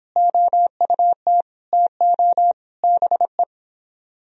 Morse code, heard as a keyed tone, about 29 wpm, 700 Hz. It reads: OUT TO BE